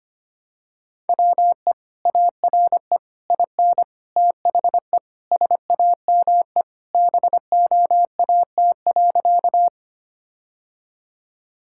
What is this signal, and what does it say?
Morse code, heard as a keyed tone, about 25 wpm, 700 Hz. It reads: WE ARE IN THE SAME BOAT.
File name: practice / 25wpm / Sentences with 3-Word Phrases - 25wpm